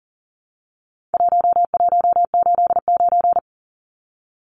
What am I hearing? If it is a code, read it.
Morse code, heard as a keyed tone, about 40 wpm, 700 Hz. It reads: 1189
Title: Numbers - 4-Digit 40wpm